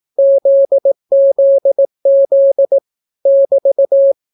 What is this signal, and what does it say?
Morse code, heard as a keyed tone, about 18 wpm, 550 Hz. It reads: ZZZ <BT>